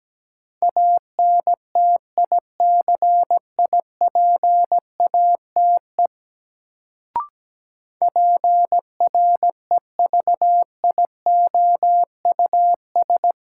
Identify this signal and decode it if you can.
Morse code, heard as a keyed tone, about 17 wpm, 700 Hz. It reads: ANTICIPATE PREVIOUS
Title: Sentences from Top 1500 Words - New Word Drill - 17wpm